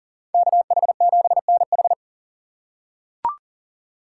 Morse code, heard as a keyed tone, about 40 wpm, 700 Hz. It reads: KH7NH
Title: USA Call Signs - 2x Character Spacing - 40wpm